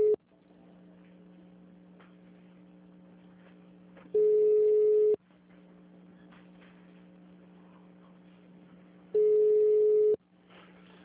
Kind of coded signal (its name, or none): telephone call progress tone